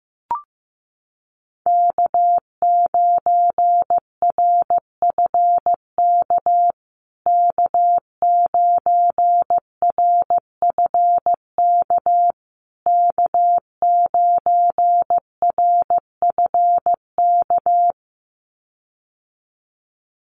Morse code, heard as a keyed tone, about 15 wpm, 700 Hz. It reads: K9RFK K9RFK K9RFK
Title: USA Call Signs Repeated 3x 15wpm